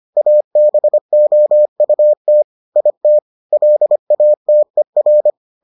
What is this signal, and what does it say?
Morse code, heard as a keyed tone, about 25 wpm, 600 Hz. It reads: ABOUT IT LATER